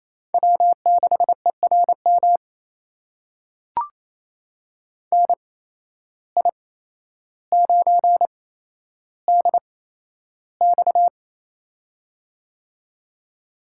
Morse code, heard as a keyed tone, about 28 wpm, spaced as for 12 wpm, 700 Hz. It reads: W6ERM NI9DX
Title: USA Call Signs - 8x Character Spacing - 28wpm